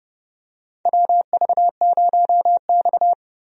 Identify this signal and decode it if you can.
Morse code, heard as a keyed tone, about 30 wpm, 700 Hz. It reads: WV0X
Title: USA Call Signs - 4 Characters - Repeated 2x - 30wpm